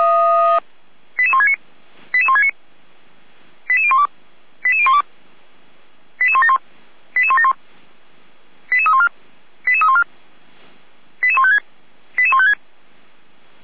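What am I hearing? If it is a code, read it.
Five-tone selective call (ZVEI), heard as quick runs of five tones, tones of about 70 ms. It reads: 88169 88169 88012 88012 88171 88171 88314 88314 88156 88156
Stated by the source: ZVEI five-tone selective call received on SDR.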